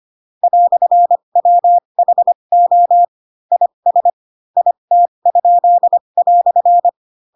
Morse code, heard as a keyed tone, about 25 wpm, spaced as for 22 wpm, 700 Hz. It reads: "WHO IS IT?"
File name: Pinocchio / Pinocchio ch6 18 25 ik1hge